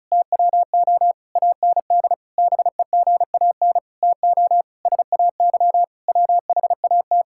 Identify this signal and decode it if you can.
Morse code, heard as a keyed tone, about 36 wpm, 700 Hz. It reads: TWO AND BEGAN TO SAY WHAT
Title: Sentences from Top 1000 Words - Review Part 5 - 35wpm